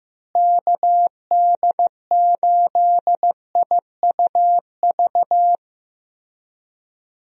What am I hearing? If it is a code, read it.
Morse code, heard as a keyed tone, about 15 wpm, 700 Hz. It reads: KD8IUV
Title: USA Call Signs - 6 Characters - Repeated 2x - 15wpm